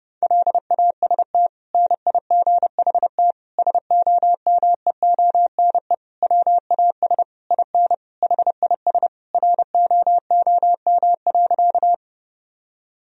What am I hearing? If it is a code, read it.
Morse code, heard as a keyed tone, about 32 wpm, 700 Hz. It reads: LAST NIGHT SOMEONE WAS IN HIS ROOM.
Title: Sentences from Top 1000 Words - Review Part 4 - 30wpm